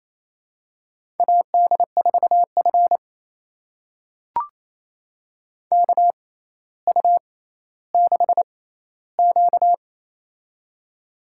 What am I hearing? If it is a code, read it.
Morse code, heard as a keyed tone, about 28 wpm, spaced as for 14 wpm, 700 Hz. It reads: AD4F KU6Q
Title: USA Call Signs - 4 Characters - 6x Character Spacing - 28wpm